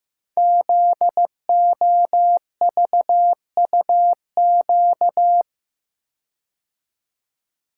Morse code, heard as a keyed tone, about 15 wpm, 700 Hz. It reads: ZOVUQ
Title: Code Groups - 5 Letters - 15wpm